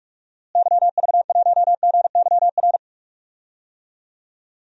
Morse code, heard as a keyed tone, about 44 wpm, 700 Hz. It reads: YV1GYR
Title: DXCC-100 Call Signs - 6 Characters - Repeated 2x - 45wpm